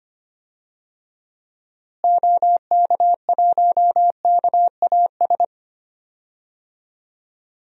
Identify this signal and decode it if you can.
Morse code, heard as a keyed tone, about 25 wpm, 700 Hz. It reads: OK1KAS
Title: DXCC-100 Call Signs - 6 Characters - Repeated 3x - 25wpm